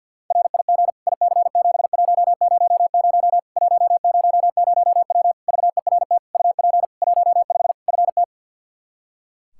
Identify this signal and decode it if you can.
Morse code, heard as a keyed tone, about 53 wpm, 700 Hz. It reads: RIG IC7100 100W VERT UP 15 FT
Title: K1USN SST QSO Practice 50wpm